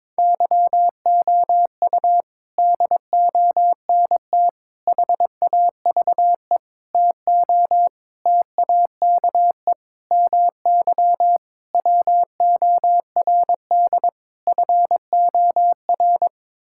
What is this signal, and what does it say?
Morse code, heard as a keyed tone, about 22 wpm, 700 Hz. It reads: YOU DONT HAVE TO TAKE MY WORD FOR
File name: practice / 22wpm / Sets of 8 Words - 22wpm